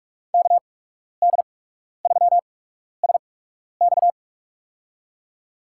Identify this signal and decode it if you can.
Morse code, heard as a keyed tone, about 47 wpm, spaced as for 12 wpm, 690 Hz. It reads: KD3SX
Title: USA Call Signs - 5 Characters - 8x Character Spacing - 45wpm